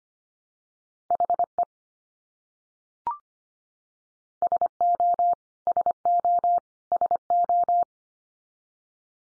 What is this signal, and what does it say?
Morse code, heard as a keyed tone, about 25 wpm, 700 Hz. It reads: HE SO SO SO
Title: Top 100 Words 25wpm x3